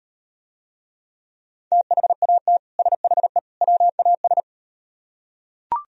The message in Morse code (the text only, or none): THAT SHE WAS